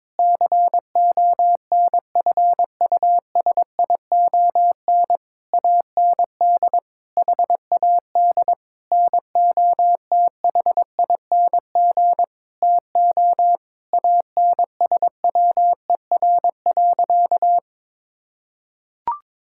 Morse code, heard as a keyed tone, about 22 wpm, 700 Hz. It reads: CONFUSION AND HAD NOTHING TO ANSWER.
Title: Sentences from Top 1600 Words 22wpm